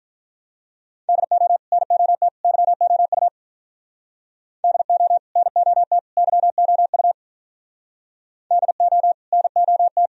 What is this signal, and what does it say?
Morse code, heard as a keyed tone, about 44 wpm, 700 Hz. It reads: DO NOT YOU DO NOT YOU DO NOT